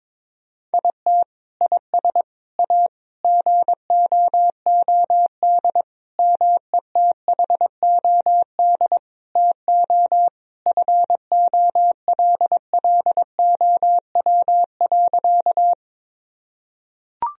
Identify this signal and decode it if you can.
Morse code, heard as a keyed tone, about 22 wpm, 700 Hz. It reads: IT IS A GOOD METHOD TO FOLLOW.
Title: Sentences from Top 1000 Words - Review Part 3 - 22wpm